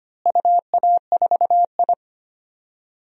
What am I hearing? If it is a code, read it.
Morse code, heard as a keyed tone, about 25 wpm, 700 Hz. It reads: UA4I
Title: DXCC-100 Call Signs - 4 Characters - Repeated 3x - 25wpm